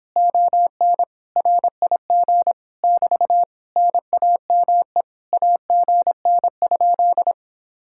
Morse code, heard as a keyed tone, about 26 wpm, 700 Hz. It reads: ON RIG <BT> NAME AGN?